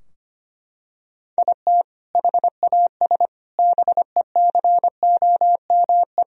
Morse code, heard as a keyed tone, about 25 wpm, 700 Hz. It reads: IT HAS BECOME
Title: Sentences from Top 1000 Words - Review Part 3 - 25wpm